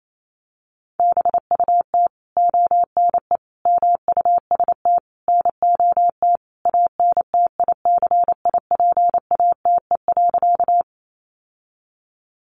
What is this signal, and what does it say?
Morse code, heard as a keyed tone, about 28 wpm, 700 Hz. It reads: BUT ONE MUST NOT ANTICIPATE.
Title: Sentences from Top 1500 Words 28wpm